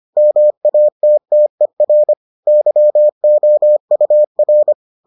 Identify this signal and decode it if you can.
Morse code, heard as a keyed tone, about 25 wpm, 600 Hz. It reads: MATTER YOUR